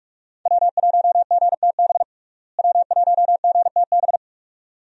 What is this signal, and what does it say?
Morse code, heard as a keyed tone, about 47 wpm, 700 Hz. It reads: W1GTB W1GTB
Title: USA Call Signs - 5 Characters - Repeated 2x - 45wpm